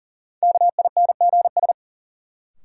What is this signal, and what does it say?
Morse code, heard as a keyed tone, about 40 wpm, 700 Hz. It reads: KINGS
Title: Sentences from Top 1000 Words - Review Part 6 - New Word Drill - 40wpm